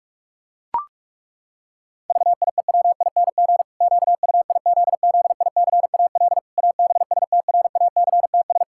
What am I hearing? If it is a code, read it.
Morse code, heard as a keyed tone, about 41 wpm, 700 Hz. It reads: VIEWING QUIZZICAL ABSTRACTS
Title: Pangram Challenge v2 - 45wpm